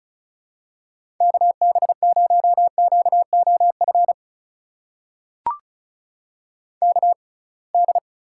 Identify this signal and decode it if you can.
Morse code, heard as a keyed tone, about 36 wpm, spaced as for 27 wpm, 700 Hz. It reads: KB0QOF KD
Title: USA Call Signs - 6 Characters - 6x Character Spacing - 35wpm